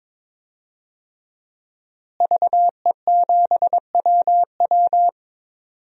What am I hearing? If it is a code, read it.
Morse code, heard as a keyed tone, about 22 wpm, 700 Hz. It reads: VE7WW
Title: DXCC-100 Call Signs - 5 Characters - Repeated 3x - 22wpm